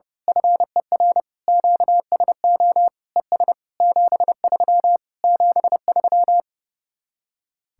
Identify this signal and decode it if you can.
Morse code, heard as a keyed tone, about 29 wpm, 700 Hz. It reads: FER QSO ES 73 73
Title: Speed-Racing Common QSO Phrases Encore - 30wpm